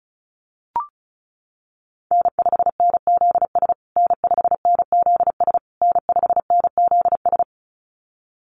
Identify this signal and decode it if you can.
Morse code, heard as a keyed tone, about 34 wpm, 700 Hz. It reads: N5NZS N5NZS N5NZS
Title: USA Call Signs Repeated 3x 35wpm